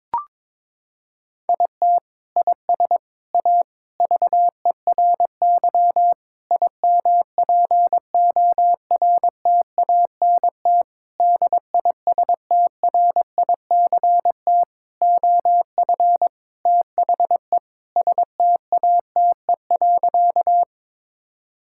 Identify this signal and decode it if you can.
Morse code, heard as a keyed tone, about 22 wpm, 700 Hz. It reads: IT IS A VERY IMPORTANT DISTRICT OF THE STATE.
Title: Sentences from Top 1600 Words 22wpm